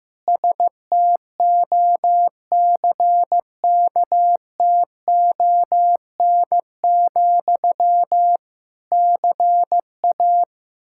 Morse code, heard as a keyed tone, about 15 wpm, 700 Hz. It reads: STOCKTON, CA
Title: Practice QSO - Exchange 1 - Beginning - 15wpm